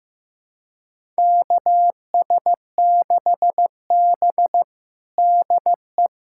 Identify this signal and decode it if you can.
Morse code, heard as a keyed tone, about 15 wpm, 700 Hz. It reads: KS6B DE
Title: USA Callsign DE Callsign - 4 Characters - 15wpm